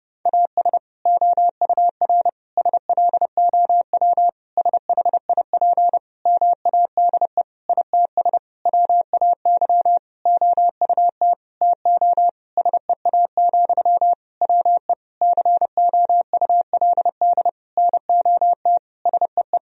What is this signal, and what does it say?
Morse code, heard as a keyed tone, about 32 wpm, 700 Hz. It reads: AS OUR SLOW SHIP MADE ITS WAY OUT TO SEA, WE COULD NOT SEE
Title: Sentences from Top 1000 Words - Review Part 2 - 30wpm